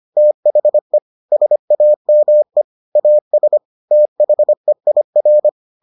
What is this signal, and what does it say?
Morse code, heard as a keyed tone, about 25 wpm, 600 Hz. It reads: THE SAME AS THEIR